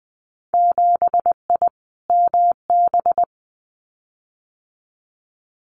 Morse code, heard as a keyed tone, about 20 wpm, 700 Hz. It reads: 7I MB
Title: Winter Field Day Exchanges Encore 20wpm